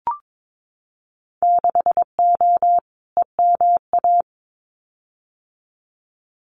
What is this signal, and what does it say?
Morse code, heard as a keyed tone, about 22 wpm, 700 Hz. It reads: 6O EMA
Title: Winter Field Day Exchanges Encore 22wpm